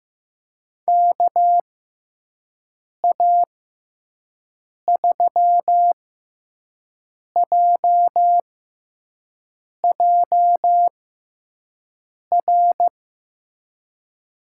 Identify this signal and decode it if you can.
Morse code, heard as a keyed tone, about 15 wpm, spaced as for 5 wpm, 700 Hz. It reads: KA3JJR
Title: USA Call Signs - 6 Characters - 6x Character Spacing - 15wpm